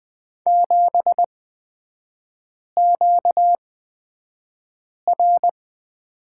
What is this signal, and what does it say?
Morse code, heard as a keyed tone, about 20 wpm, spaced as for 5 wpm, 700 Hz. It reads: 7QR
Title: 250 Lesson 40e - BK - Three Characters 20wpm